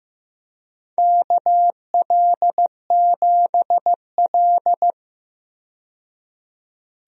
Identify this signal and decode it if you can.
Morse code, heard as a keyed tone, about 16 wpm, 700 Hz. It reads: KL7L
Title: DXCC-100 Call Signs - 4 Characters - 15wpm